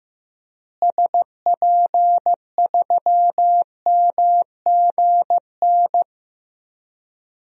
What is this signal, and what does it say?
Morse code, heard as a keyed tone, about 15 wpm, 700 Hz. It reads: SP3MGN
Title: DXCC-100 Call Signs - 6 Characters - Repeated 3x - 15wpm